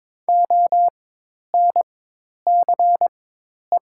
Morse code, heard as a keyed tone, about 22 wpm, spaced as for 10 wpm, 700 Hz. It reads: ONCE
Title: 4x Character Spacing - Top 500 Words - 22wpm